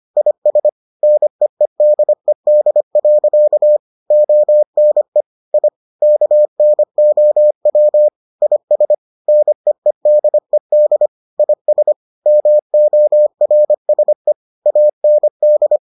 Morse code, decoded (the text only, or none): IS NEEDED. ONE I KNOW IS NEEDED IS MORSE AND